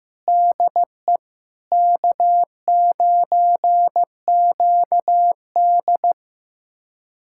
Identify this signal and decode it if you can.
Morse code, heard as a keyed tone, about 15 wpm, 700 Hz. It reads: DE K9QD
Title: USA Callsign DE Callsign - 4 Characters - 15wpm